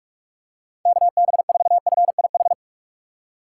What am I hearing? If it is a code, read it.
Morse code, heard as a keyed tone, about 46 wpm, 700 Hz. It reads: KB4FIH